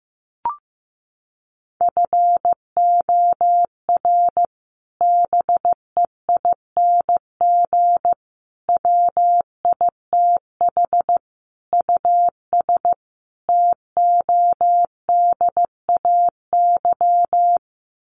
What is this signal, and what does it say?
Morse code, heard as a keyed tone, about 15 wpm, 700 Hz. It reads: FOR BEING WITH US TODAY
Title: Sets of 5 Words Encore2 15wpm